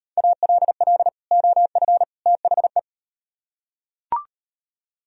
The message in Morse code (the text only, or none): ALL OF THE